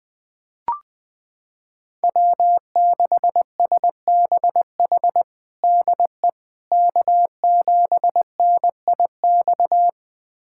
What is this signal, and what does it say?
Morse code, heard as a keyed tone, about 20 wpm, 700 Hz. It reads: W6SBH DE K7NIX
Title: USA Callsign DE Callsign 20wpm